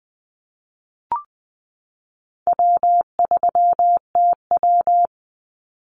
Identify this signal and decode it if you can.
Morse code, heard as a keyed tone, about 20 wpm, 700 Hz. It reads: W3TW
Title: Call Signs - 20wpm